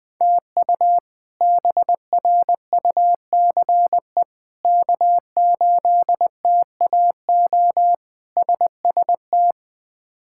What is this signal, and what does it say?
Morse code, heard as a keyed tone, about 20 wpm, 700 Hz. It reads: TU BRUCE K8TAO SST
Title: K1USN SST QSO Practice - 20wpm